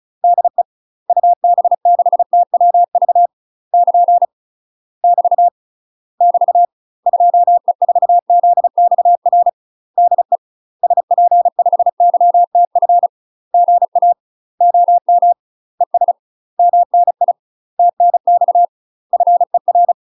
Morse code, heard as a keyed tone, about 36 wpm, 700 Hz. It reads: DE UB6TWV <KN> <BT> <BT> 2E4ZXR DE SP5YTF GA OM ES MNI TNX FER
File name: UK MASTER QSOs TRAINING/35WPM